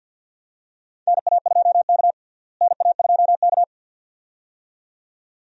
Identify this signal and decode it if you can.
Morse code, heard as a keyed tone, about 50 wpm, 700 Hz. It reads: NA2K NA2K
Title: USA Call Signs - 4 Characters - Repeated 2x - 50wpm